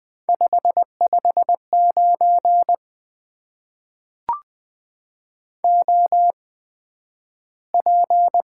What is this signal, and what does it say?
Morse code, heard as a keyed tone, about 20 wpm, spaced as for 11 wpm, 700 Hz. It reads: 559 OP
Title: Top 90 QSO Elements - 8x Character Spacing - 20wpm